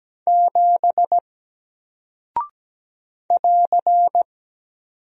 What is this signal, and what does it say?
Morse code, heard as a keyed tone, about 17 wpm, 700 Hz. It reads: 7 <AR>
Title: Single Character (Letter:Number:Punctuation:Prosign) 17wpm